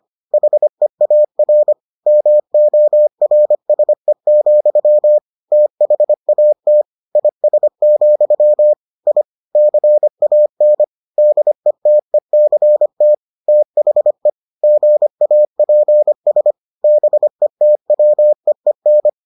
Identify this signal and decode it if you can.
Morse code, heard as a keyed tone, about 25 wpm, 600 Hz. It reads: HEAR MORSE, THAT IS, I CAN DETECT THE GAPS BETWEEN